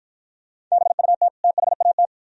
Morse code, heard as a keyed tone, about 52 wpm, 700 Hz. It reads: BUT THAT